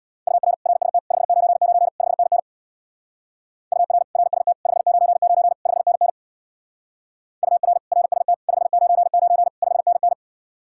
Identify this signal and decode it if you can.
Morse code, heard as a keyed tone, about 67 wpm, 700 Hz. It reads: UR RST 599 5NN UR RST 599 5NN UR RST 599 5NN